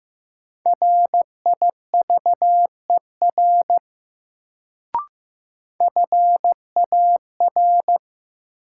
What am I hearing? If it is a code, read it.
Morse code, heard as a keyed tone, about 15 wpm, 700 Hz. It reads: RIVER FAR
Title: Sentences from Top 300 Words - New Word Drill - 15wpm